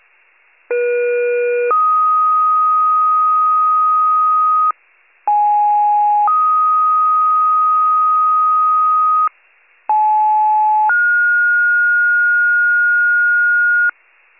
Two-tone page, heard as a one-second tone, then a three-second tone, about 480 then 1250 Hz, about 830 then 1250 Hz, about 860 then 1430 Hz. It is heard from 0.7 to 13.9 s.
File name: wcalert